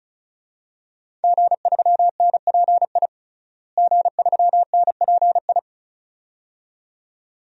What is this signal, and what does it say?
Morse code, heard as a keyed tone, about 34 wpm, 700 Hz. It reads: G3NPI G3NPI
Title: DXCC-100 Call Signs - 5 Characters - Repeated 2x - 35wpm